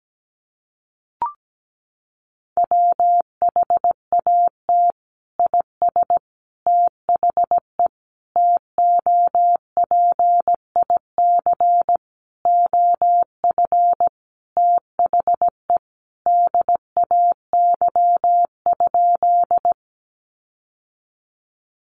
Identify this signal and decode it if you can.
Morse code, heard as a keyed tone, about 17 wpm, 700 Hz. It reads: WHAT IS THE TOPIC OF THE DAY?
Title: Sentences from Top 1200 Words 17wpm